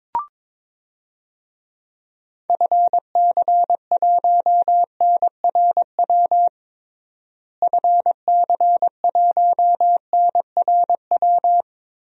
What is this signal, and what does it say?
Morse code, heard as a keyed tone, about 22 wpm, 700 Hz. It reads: FC1NRW FC1NRW
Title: DXCC-100 Call Signs - 6 Characters - Repeated 3x - 22wpm